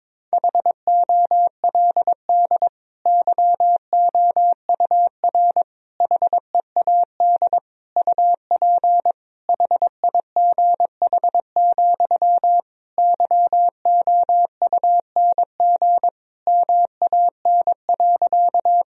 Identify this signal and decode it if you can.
Morse code, heard as a keyed tone, about 22 wpm, 700 Hz. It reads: HOLD YOUR HEAD UP HIGH, YOUNG MAN.
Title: Sentences from Top 1000 Words - Review Part 2 - 22wpm